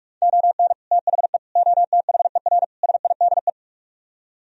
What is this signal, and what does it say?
Morse code, heard as a keyed tone, about 67 wpm, spaced as for 55 wpm, 700 Hz. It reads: ON THE OTHER SIDE
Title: Speed-Racing - Sets of 4 Words 45wpm